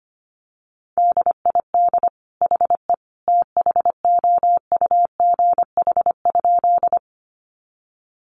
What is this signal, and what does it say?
Morse code, heard as a keyed tone, about 25 wpm, 700 Hz. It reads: DID HE THOUGH?
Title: Sentences from Top 400 Words 25wpm